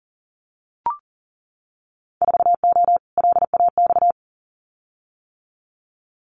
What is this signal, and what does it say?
Morse code, heard as a keyed tone, about 40 wpm, 700 Hz. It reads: 4O LAX
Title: Winter Field Day Exchanges Encore 40wpm